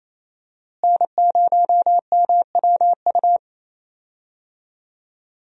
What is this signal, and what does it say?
Morse code, heard as a keyed tone, about 28 wpm, 700 Hz. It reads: N0MWU
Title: USA Call Signs - 5 Characters - Repeated 3x - 28wpm